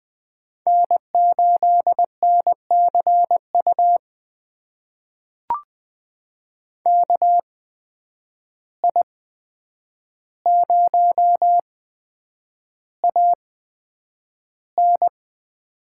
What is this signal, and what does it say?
Morse code, heard as a keyed tone, about 20 wpm, spaced as for 9 wpm, 700 Hz. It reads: N8NCU KI0AN
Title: USA Call Signs - 5 Characters - 8x Character Spacing - 20wpm